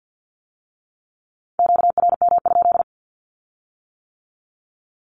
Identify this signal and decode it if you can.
Morse code, heard as a keyed tone, about 50 wpm, 700 Hz. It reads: QRM?
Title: Q-Codes with Spoken Meaning - 50wpm